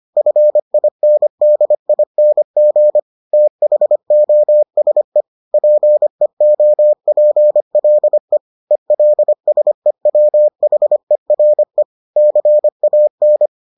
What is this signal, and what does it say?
Morse code, heard as a keyed tone, about 25 wpm, 600 Hz. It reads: FINDING THOSE PEOPLE ELSEWHERE CAN